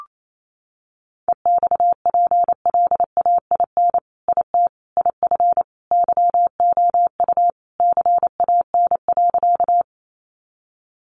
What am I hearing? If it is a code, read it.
Morse code, heard as a keyed tone, about 28 wpm, 700 Hz. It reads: EXPLAIN IT IF YOU CAN.
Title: Sentences from Top 1100 Words 28wpm